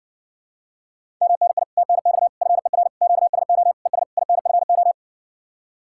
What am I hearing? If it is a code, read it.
Morse code, heard as a keyed tone, about 60 wpm, 700 Hz. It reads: MNI TNX FER QSO ES INFO